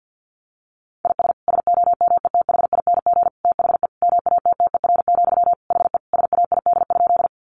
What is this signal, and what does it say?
Morse code, heard as a keyed tone, about 50 wpm, 700 Hz. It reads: IS SOMETHING THE MATTER, HE SAID?
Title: Sentences from Top 700 Words 50wpm